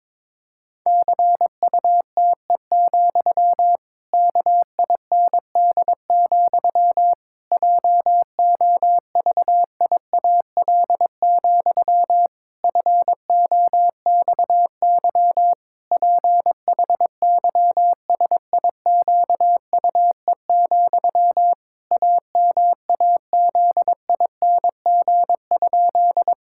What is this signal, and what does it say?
Morse code, heard as a keyed tone, about 22 wpm, 700 Hz. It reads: CUTE, KIND, JOVIAL, FOXY PHYSIQUE, AMAZING?
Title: Pangram Challenge v2 - 22wpm